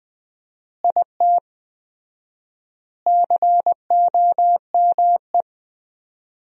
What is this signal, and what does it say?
Morse code, heard as a keyed tone, about 20 wpm, 700 Hz. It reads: IT COME
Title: 4x Word-Spacing - Sentences from Top 100 Words - 20wpm